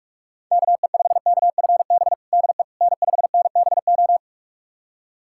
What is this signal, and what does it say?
Morse code, heard as a keyed tone, about 46 wpm, 700 Hz. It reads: KE5KFB DE N5NBO
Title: USA Callsign DE Callsign 45wpm